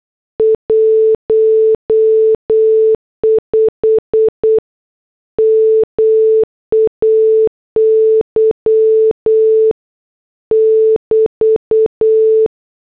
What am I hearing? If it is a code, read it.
Morse code, heard as a keyed tone, about 8 wpm, 440 Hz. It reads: 15 MAY <BT>